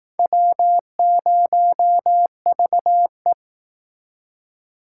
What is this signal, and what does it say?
Morse code, heard as a keyed tone, about 18 wpm, 700 Hz. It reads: W0VE
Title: Speed Racing - USA 1x2 Call Signs 15wpm